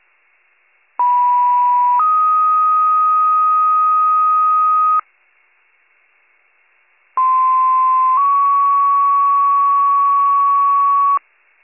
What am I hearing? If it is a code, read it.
Two-tone page, heard as a one-second tone, then a three-second tone, about 980 then 1250 Hz, about 1040 then 1120 Hz. It is heard from 1.0 to 11.2 s.